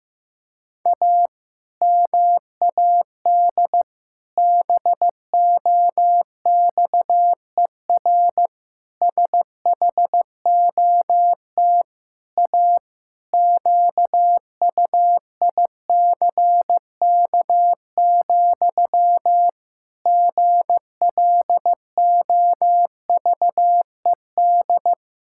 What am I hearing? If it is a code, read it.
Morse code, heard as a keyed tone, about 15 wpm, 700 Hz. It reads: A MAD BOXER SHOT A QUICK, GLOVED
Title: Pangram Challenge v2 - 15wpm